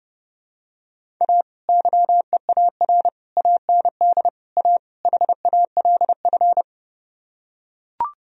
Morse code, heard as a keyed tone, about 30 wpm, 700 Hz. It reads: A YEAR AND A HALF